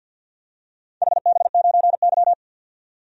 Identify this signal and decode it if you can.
Morse code, heard as a keyed tone, about 47 wpm, 700 Hz. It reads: HB9Y
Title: DXCC-100 Call Signs - 4 Characters - 50wpm